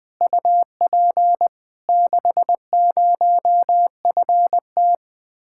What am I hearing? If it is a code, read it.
Morse code, heard as a keyed tone, about 20 wpm, 700 Hz. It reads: UP 60FT
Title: Speed-Racing Common QSO Phrases - 17wpm